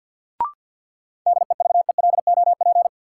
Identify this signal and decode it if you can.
Morse code, heard as a keyed tone, about 53 wpm, 700 Hz. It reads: DEVELOP
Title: Sentences from Top 1000 Words - Review Part 2 - New Word Drill - 50wpm